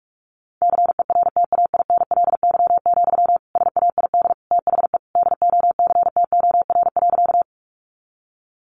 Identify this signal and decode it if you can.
Morse code, heard as a keyed tone, about 44 wpm, 700 Hz. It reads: CERTAINLY, SAID THE DOCTOR.
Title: Sentences from Top 1500 Words 45wpm